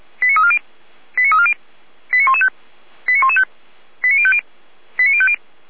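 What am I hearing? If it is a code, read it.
Five-tone selective call (ZVEI), heard as quick runs of five tones, tones of about 70 ms. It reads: 89340 89340 89115 89115 89050 89050
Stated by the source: ZVEI five-tone selective call received on SDR.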